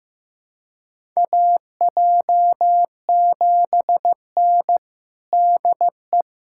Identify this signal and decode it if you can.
Morse code, heard as a keyed tone, about 15 wpm, 700 Hz. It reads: AJ7N DE
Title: USA Callsign DE Callsign - 4 Characters - 15wpm